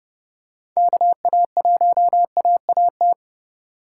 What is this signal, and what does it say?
Morse code, heard as a keyed tone, about 28 wpm, 700 Hz. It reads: KA1AAT